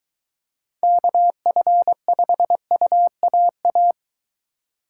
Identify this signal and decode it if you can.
Morse code, heard as a keyed tone, about 20 wpm, 700 Hz. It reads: KF5UAA